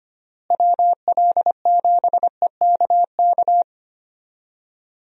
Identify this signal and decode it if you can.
Morse code, heard as a keyed tone, about 25 wpm, 700 Hz. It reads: WL7EKK
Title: USA Call Signs - 6 Characters - Repeated 3x - 25wpm